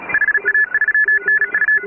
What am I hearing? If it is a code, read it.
Morse code, heard as a keyed tone, about 36 wpm, 1700 Hz. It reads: HA3NU